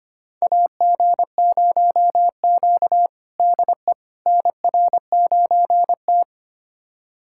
Morse code, heard as a keyed tone, about 25 wpm, 700 Hz. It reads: AG0Q DE NR9T
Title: USA Callsign DE Callsign - 4 Characters - 25wpm